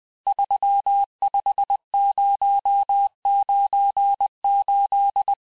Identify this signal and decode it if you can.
Morse code, heard as a keyed tone, about 20 wpm, 800 Hz. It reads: 35098
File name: cours-1584-teste-chiffre-vitesse-19mpm